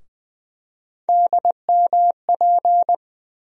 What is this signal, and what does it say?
Morse code, heard as a keyed tone, about 21 wpm, 700 Hz. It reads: DMP
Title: USA Call Sign Postfixes - 3 Letters 20wpm